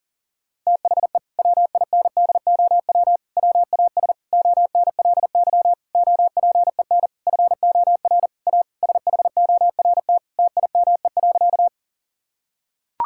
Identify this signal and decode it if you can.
Morse code, heard as a keyed tone, about 40 wpm, 700 Hz. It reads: THE WINDOW WAS ONLY OPEN FOR A SHORT TIME.
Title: Sentences from Top 1600 Words 40wpm